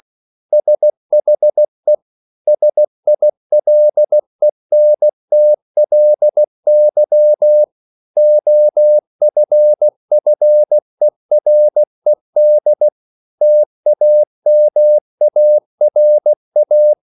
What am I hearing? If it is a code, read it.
Morse code, heard as a keyed tone, about 16 wpm, 600 Hz. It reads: SHE SILENTLY OFFERED TAMARA